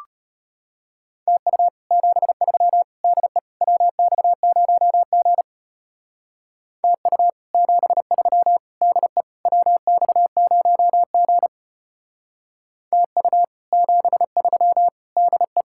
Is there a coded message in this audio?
Yes: Morse code from 1.3 to 15.6 s, heard as a keyed tone, about 33 wpm, 700 Hz. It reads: TU 73 DE WX0G TU 73 DE WX0G TU 73 DE